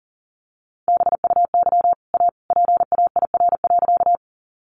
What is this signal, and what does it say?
Morse code, heard as a keyed tone, about 40 wpm, 700 Hz. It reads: BUY A PAIR.
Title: Sentences from Top 1200 Words 40wpm